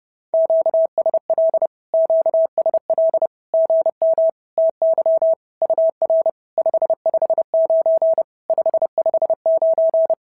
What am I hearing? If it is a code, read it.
Morse code, heard as a keyed tone, about 28 wpm, 650 Hz. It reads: QSL QSL GM TY UR 559 559
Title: SOTA QSO Practice 30wpm